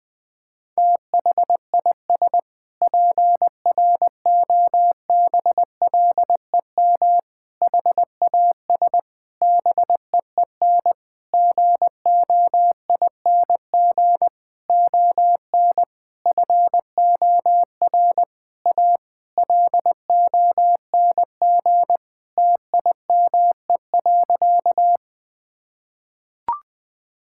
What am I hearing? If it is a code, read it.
Morse code, heard as a keyed tone, about 20 wpm, 700 Hz. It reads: THIS PROBLEM HAS BEEN GOING ON FOR A LONG TIME.
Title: Sentences with 8-Word Phrases - 20wpm